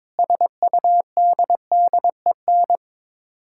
Morse code, heard as a keyed tone, about 22 wpm, 700 Hz. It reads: SUDDEN
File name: practice / 22wpm / Sets of 6 Words - 22wpm